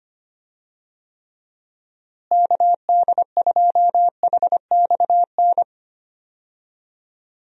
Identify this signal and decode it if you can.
Morse code, heard as a keyed tone, about 25 wpm, 700 Hz. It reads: KD2HXN
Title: USA Call Signs - 6 Characters - Repeated 3x - 25wpm